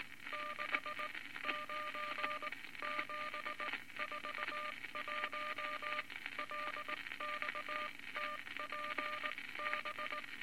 Morse code, heard as a keyed tone, about 19 wpm, spaced as for 16 wpm, 1200 Hz, with noise. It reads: 69741 LXTPB
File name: Morse Code Records / 19 B LCM 167 Twelve Words Per Minute